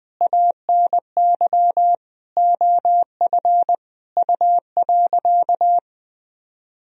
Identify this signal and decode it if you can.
Morse code, heard as a keyed tone, about 21 wpm, 700 Hz. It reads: ANY OF U.
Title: Sentences from Top 400 Words and Abbreviations 20wpm